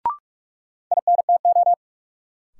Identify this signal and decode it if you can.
Morse code, heard as a keyed tone, about 46 wpm, 700 Hz. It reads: INTO
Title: Sets of 2 Words - New Word Drill - 45wpm